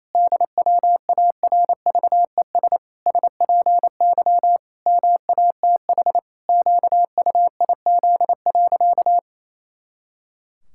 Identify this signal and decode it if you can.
Morse code, heard as a keyed tone, about 28 wpm, 700 Hz. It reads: DWARVES SPY MATH QUIZ.
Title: Pangram Challenge v2 - 28wpm